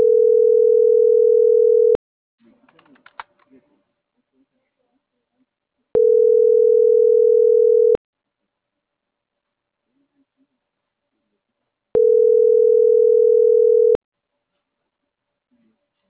Ringback tone (North America), 440 + 480 Hz.